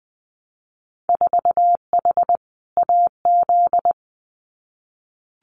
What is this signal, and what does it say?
Morse code, heard as a keyed tone, about 20 wpm, 700 Hz. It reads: 4H AZ
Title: Winter Field Day Exchanges Encore 20wpm